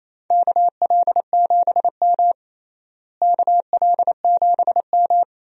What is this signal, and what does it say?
Morse code, heard as a keyed tone, about 28 wpm, 700 Hz. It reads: KL7M KL7M
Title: DXCC-100 Call Signs - 4 Characters - Repeated 3x - 28wpm